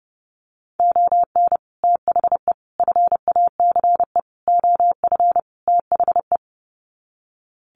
Morse code, heard as a keyed tone, about 30 wpm, 700 Hz. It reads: ON THE FACE OF THE
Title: Sets of 5 Words Encore2 30wpm